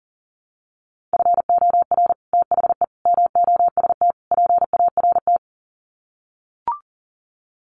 Morse code, heard as a keyed tone, about 40 wpm, 700 Hz. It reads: FOR THE MOST PART
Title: Sets of 4 Words 40wpm